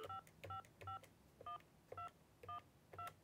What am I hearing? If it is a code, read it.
DTMF, heard as telephone touch tones, tones of about 120 ms. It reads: 5551212